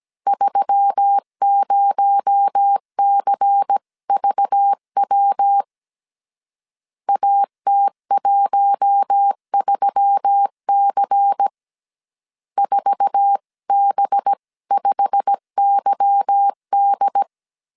Morse code, heard as a keyed tone, about 17 wpm, 800 Hz. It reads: 30CVW AT13C 4B5YD
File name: CW / DET / 1 9 1500C S C 16wpm 2909